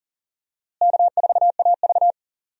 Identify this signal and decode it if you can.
Morse code, heard as a keyed tone, about 40 wpm, 700 Hz. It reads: K4AV
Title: USA Call Signs - 4 Characters - Repeated 3x - 40wpm